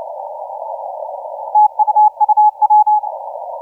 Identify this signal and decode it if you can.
Morse code, heard as a keyed tone, about 29 wpm, 850 Hz. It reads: TUUW
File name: A35UD 0956UTC 17m